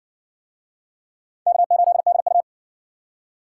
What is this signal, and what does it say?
Morse code, heard as a keyed tone, about 60 wpm, 700 Hz. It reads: K8DU